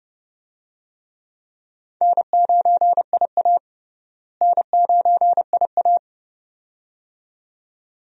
Morse code, heard as a keyed tone, about 32 wpm, 700 Hz. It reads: N9IA N9IA
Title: USA Call Signs - 4 Characters - Repeated 2x - 30wpm